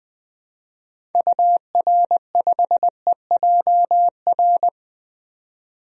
Morse code, heard as a keyed tone, about 20 wpm, 700 Hz. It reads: UR5EJR
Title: DXCC-100 Call Signs - 6 Characters - 20wpm